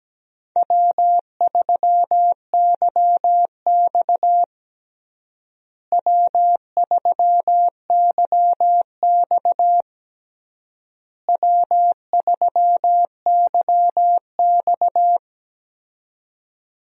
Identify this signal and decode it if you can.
Morse code, heard as a keyed tone, about 17 wpm, 700 Hz. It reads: W3YX W3YX W3YX
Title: USA Call Signs - 4 Characters - Repeated 3x - 17wpm